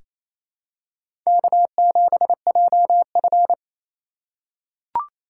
Morse code, heard as a keyed tone, about 28 wpm, 700 Hz. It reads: K7JF
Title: USA Call Signs - 4 Characters - 2x Character Spacing - 28wpm